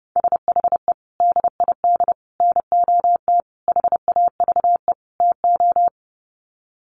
Morse code, heard as a keyed tone, about 30 wpm, 700 Hz. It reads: SHE DID NOT HAVE TO
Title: Sets of 5 Words Encore2 30wpm